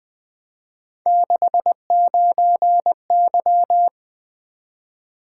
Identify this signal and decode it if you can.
Morse code, heard as a keyed tone, about 20 wpm, 700 Hz. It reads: 69Y
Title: Three Letter-Number Combinations - 20wpm